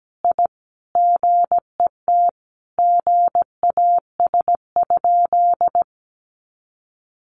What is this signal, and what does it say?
Morse code, heard as a keyed tone, about 17 wpm, 700 Hz. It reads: I GET GAS?
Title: Sentences from Top 1100 Words 17wpm